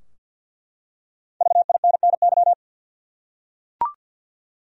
Morse code, heard as a keyed tone, about 51 wpm, 700 Hz. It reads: VINNY
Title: USA Common Names - Max 5 Letters - 50wpm